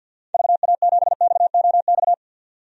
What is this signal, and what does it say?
Morse code, heard as a keyed tone, about 48 wpm, 700 Hz. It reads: VA7XOX
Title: DXCC-100 Call Signs - 6 Characters - 50wpm